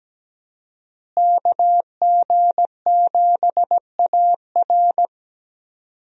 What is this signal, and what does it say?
Morse code, heard as a keyed tone, about 17 wpm, 700 Hz. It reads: KG7AR
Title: USA Call Signs - 5 Characters - Repeated 3x - 17wpm